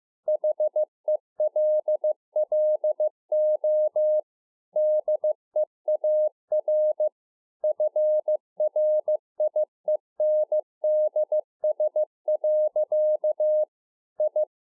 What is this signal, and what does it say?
Morse code, heard as a keyed tone, about 15 wpm, 610 Hz. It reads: HELLO DEAR FRIENDS. I